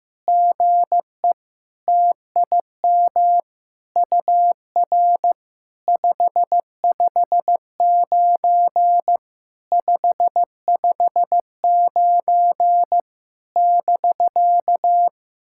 Morse code, heard as a keyed tone, about 15 wpm, 700 Hz. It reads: GE TIM UR 559 559 <BK>
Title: SOTA QSO Practice 15wpm